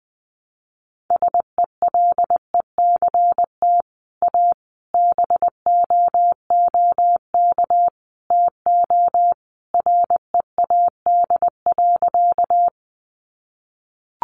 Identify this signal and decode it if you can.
Morse code, heard as a keyed tone, about 20 wpm, 700 Hz. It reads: SELECT A BOOK TO READ.
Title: Sentences from Top 1300 Words 20wpm